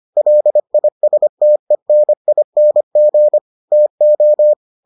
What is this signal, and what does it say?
Morse code, heard as a keyed tone, about 25 wpm, 600 Hz. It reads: LISTENING TO